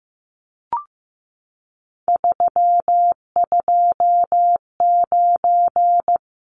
Morse code, heard as a keyed tone, about 15 wpm, 700 Hz. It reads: 329
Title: Numbers - 3-Digit 15wpm